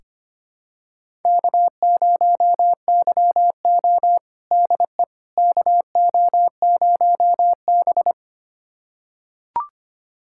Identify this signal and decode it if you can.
Morse code, heard as a keyed tone, about 25 wpm, 700 Hz. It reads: K0YO DE KO0B
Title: USA Callsign DE Callsign - 4 Characters - 25wpm